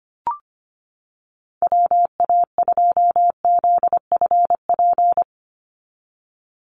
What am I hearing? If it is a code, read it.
Morse code, heard as a keyed tone, about 25 wpm, 700 Hz. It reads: WA2ZFP
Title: USA Call Signs 25wpm